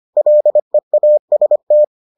Morse code, heard as a keyed tone, about 25 wpm, 600 Hz. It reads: LEAST